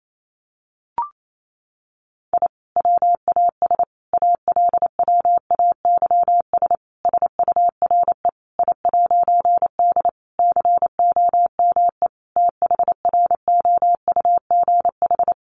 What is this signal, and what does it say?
Morse code, heard as a keyed tone, about 28 wpm, 700 Hz. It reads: I WAS ALWAYS SURE I'D COME THROUGH
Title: A Sound Of Thunder 28wpm